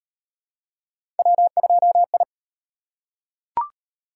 Morse code, heard as a keyed tone, about 34 wpm, 700 Hz. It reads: W2I